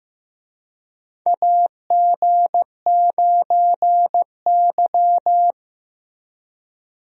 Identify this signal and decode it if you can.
Morse code, heard as a keyed tone, about 15 wpm, 700 Hz. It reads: AG9Y
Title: USA Call Signs - 4 Characters - Repeated 2x - 15wpm